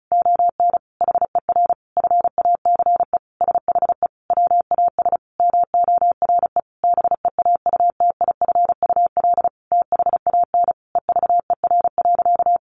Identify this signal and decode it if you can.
Morse code, heard as a keyed tone, about 34 wpm, 700 Hz. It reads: ON HER FACE SHE WAS MORE BEAUTIFUL THAN EVER.
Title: Sentences from Top 1200 Words 35wpm